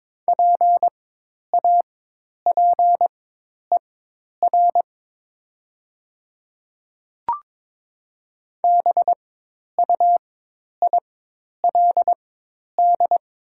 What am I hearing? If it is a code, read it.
Morse code, heard as a keyed tone, about 22 wpm, spaced as for 10 wpm, 700 Hz. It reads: PAPER BUILD
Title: 4x Character Spacing - Top 500 Words - 22wpm